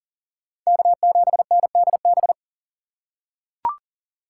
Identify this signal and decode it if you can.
Morse code, heard as a keyed tone, about 40 wpm, 700 Hz. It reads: K7NDB
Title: USA Call Signs - 5 Characters - 4x Character Spacing - 40wpm